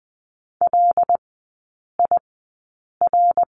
Morse code, heard as a keyed tone, about 20 wpm, spaced as for 8 wpm, 690 Hz. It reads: LIR
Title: 133 Lesson 22e - Question Mark - Three Characters 20wpm